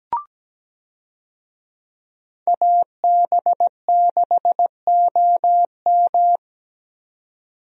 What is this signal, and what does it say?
Morse code, heard as a keyed tone, about 17 wpm, 700 Hz. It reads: AB6OM
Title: USA Call Signs - 5 Characters - 17wpm